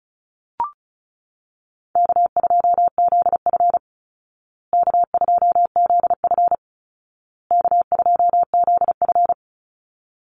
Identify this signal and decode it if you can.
Morse code, heard as a keyed tone, about 34 wpm, 700 Hz. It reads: K2ZF K2ZF K2ZF
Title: Easy Call Signs Repeated 3x - 35wpm